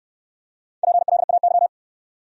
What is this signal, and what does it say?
Morse code, heard as a keyed tone, about 68 wpm, 700 Hz. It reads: PLAY